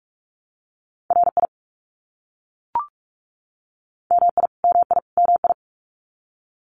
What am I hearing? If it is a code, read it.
Morse code, heard as a keyed tone, about 44 wpm, 700 Hz. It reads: RI MI MI MI
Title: US States Repeated 3x 45wpm